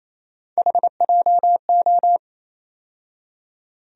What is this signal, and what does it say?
Morse code, heard as a keyed tone, about 28 wpm, 700 Hz. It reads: HJO
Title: USA Call Sign Postfixes - 3 Letters 28wpm